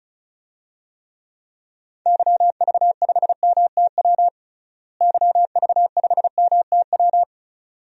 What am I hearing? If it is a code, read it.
Morse code, heard as a keyed tone, about 34 wpm, 700 Hz. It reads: YV5MTW YV5MTW
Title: DXCC-100 Call Signs - 6 Characters - Repeated 2x - 35wpm